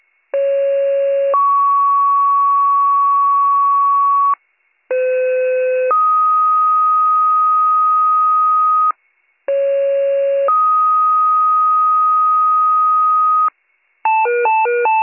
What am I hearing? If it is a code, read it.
Two-tone page, heard as a one-second tone, then a three-second tone, about 550 then 1120 Hz, about 510 then 1250 Hz, about 550 then 1250 Hz. It is heard from 0.3 to 13.5 s.